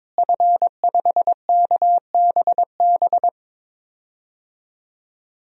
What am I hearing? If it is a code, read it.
Morse code, heard as a keyed tone, about 23 wpm, 700 Hz. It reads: F5KBB
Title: DXCC-100 Call Signs - 5 Characters - Repeated 3x - 22wpm